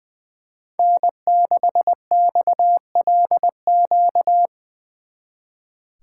Morse code, heard as a keyed tone, about 24 wpm, 700 Hz. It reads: N6XLQ
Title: Speed Racing - USA 1x3 Call Signs 20wpm